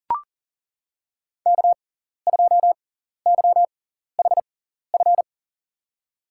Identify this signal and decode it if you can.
Morse code, heard as a keyed tone, about 40 wpm, spaced as for 14 wpm, 700 Hz. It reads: K2YHF
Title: USA Call Signs - 5 Characters - 6x Character Spacing - 40wpm